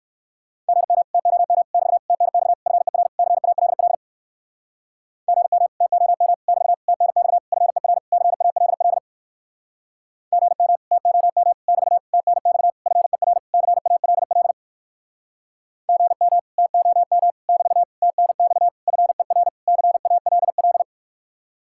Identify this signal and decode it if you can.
Morse code, heard as a keyed tone, about 59 wpm, 700 Hz. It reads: GM TOM <BT> TNX FER CALL GM TOM <BT> TNX FER CALL GM TOM <BT> TNX FER CALL GM TOM <BT> TNX FER CALL